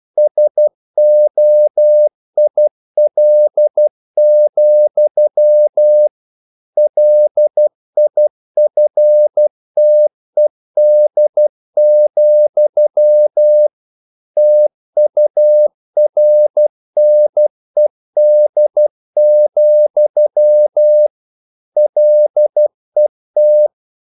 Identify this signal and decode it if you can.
Morse code, heard as a keyed tone, about 12 wpm, 600 Hz. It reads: SOIL, LIFTED, TURNED, LET
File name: mp3 files / the gardener - 12wpm